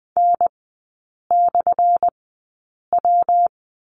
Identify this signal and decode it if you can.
Morse code, heard as a keyed tone, about 20 wpm, spaced as for 8 wpm, 700 Hz. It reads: N/W
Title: 178 Lesson 29e - Slash - Three Characters 20wpm